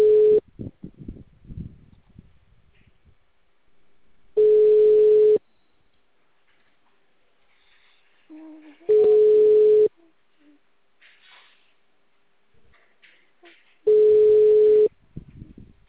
Ringback tone (Europe, 425 Hz).